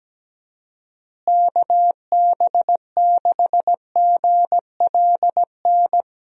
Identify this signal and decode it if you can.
Morse code, heard as a keyed tone, about 17 wpm, 700 Hz. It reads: KB6GLN
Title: USA Call Signs - 6 Characters - Repeated 3x - 17wpm